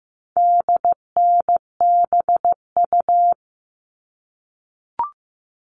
Morse code, heard as a keyed tone, about 15 wpm, 700 Hz. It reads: DNBU
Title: All Cut Numbers - 4-Digit 15wpm